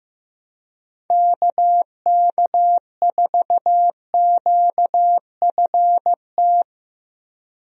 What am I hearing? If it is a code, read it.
Morse code, heard as a keyed tone, about 15 wpm, 700 Hz. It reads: KK4QFT
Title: USA Call Signs - 6 Characters - Repeated 3x - 15wpm